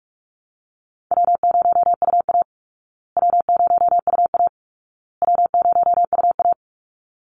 Morse code, heard as a keyed tone, about 44 wpm, 700 Hz. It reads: W0UA W0UA W0UA
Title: Easy Call Signs Repeated 3x - 45wpm